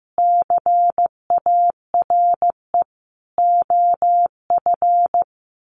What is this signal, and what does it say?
Morse code, heard as a keyed tone, about 15 wpm, 700 Hz. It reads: CARE OF
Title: Sentences from Top 1500 Words 15wpm